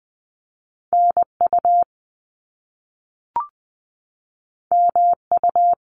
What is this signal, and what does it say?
Morse code, heard as a keyed tone, about 20 wpm, 700 Hz. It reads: NU MU
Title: Top Two Letter Combinations in Words 20wpm